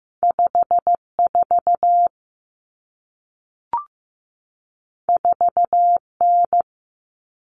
Morse code, heard as a keyed tone, about 15 wpm, 700 Hz. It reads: 54 4N
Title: Common Cut Numbers - 2-Digit 15wpm